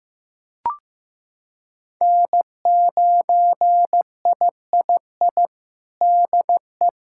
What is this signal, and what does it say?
Morse code, heard as a keyed tone, about 15 wpm, 700 Hz. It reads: N9III DE
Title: Practice QSO - Exchange 4 - Ending - 15wpm